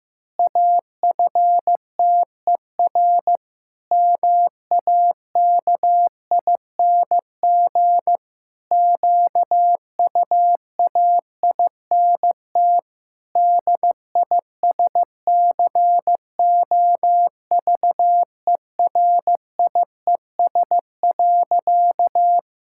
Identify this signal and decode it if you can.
Morse code, heard as a keyed tone, about 15 wpm, 700 Hz. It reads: AFTER MAKING QUAINT DISCOVERIES.
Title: Pangram Challenge v2 - 15wpm